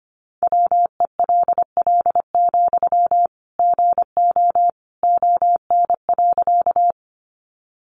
Morse code, heard as a keyed tone, about 25 wpm, 700 Hz. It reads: WELL, GO ON.
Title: Sentences from Top 200 Words 25wpm